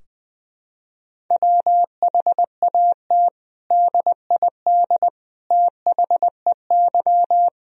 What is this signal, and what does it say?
Morse code, heard as a keyed tone, about 20 wpm, 700 Hz. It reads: WHAT DID THEY
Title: Sentences from Top 1000 Words - Review Part 4 - 20wpm